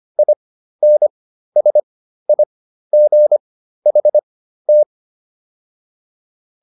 Morse code, heard as a keyed tone, about 25 wpm, spaced as for 13 wpm, 600 Hz. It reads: INSIGHT